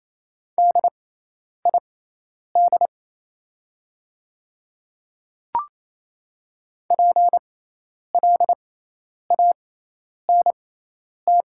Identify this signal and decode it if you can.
Morse code, heard as a keyed tone, about 27 wpm, spaced as for 10 wpm, 700 Hz. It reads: DID PLANT
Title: 6x Character Spacing - Top 300 Words - 28wpm